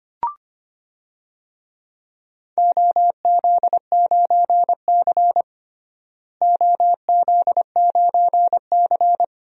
Morse code, heard as a keyed tone, about 25 wpm, 700 Hz. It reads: OZ9C OZ9C
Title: DXCC-100 Call Signs - 4 Characters - Repeated 2x - 25wpm